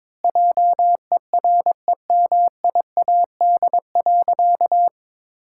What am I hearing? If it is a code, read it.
Morse code, heard as a keyed tone, about 22 wpm, 700 Hz. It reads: JEREMIAD.
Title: Pangram Challenge v2 - 22wpm